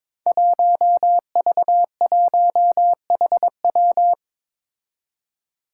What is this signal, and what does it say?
Morse code, heard as a keyed tone, about 23 wpm, 700 Hz. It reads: 1V1HW
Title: Five Challenging Letter-Number Combinations 22wpm